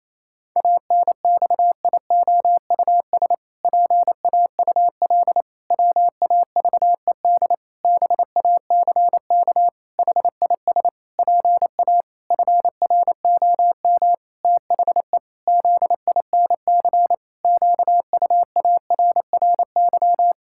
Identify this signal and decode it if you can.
Morse code, heard as a keyed tone, about 27 wpm, 700 Hz. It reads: ANXIOUS PAUL WAVED BACK HIS PA FROM THE ZINC QUARRY
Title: Pangram Challenge v2 - 28wpm